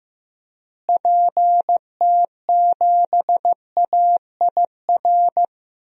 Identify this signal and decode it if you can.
Morse code, heard as a keyed tone, about 15 wpm, 700 Hz. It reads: PT7AIR
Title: DXCC-100 Call Signs - 6 Characters - Repeated 2x - 15wpm